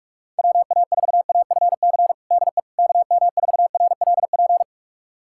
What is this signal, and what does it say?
Morse code, heard as a keyed tone, about 41 wpm, 700 Hz. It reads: WA4AFC DE KM4RLP
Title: USA Callsign DE Callsign - 6 Characters - 45wpm